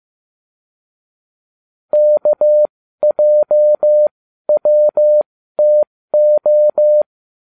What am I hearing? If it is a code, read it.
Morse code, heard as a keyed tone, about 15 wpm, spaced as for 12 wpm, 600 Hz. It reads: KJWTO